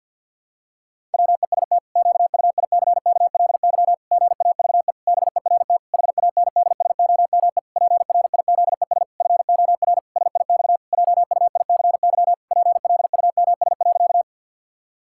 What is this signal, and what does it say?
Morse code, heard as a keyed tone, about 50 wpm, 700 Hz. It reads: WEST QUICKLY GAVE BERT HANDSOME PRIZES FOR SIX JUICY PLUMS.
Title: Pangram Challenge v2 - 50wpm